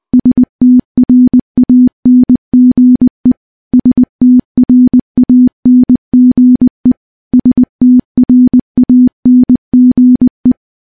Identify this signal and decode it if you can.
Morse code, heard as a keyed tone, about 21 wpm, 250 Hz. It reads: STRANGE STRANGE STRANGE
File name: morse-training-20WPM